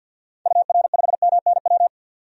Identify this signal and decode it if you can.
Morse code, heard as a keyed tone, about 48 wpm, 700 Hz. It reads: UR5MNW
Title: DXCC-100 Call Signs - 6 Characters - 50wpm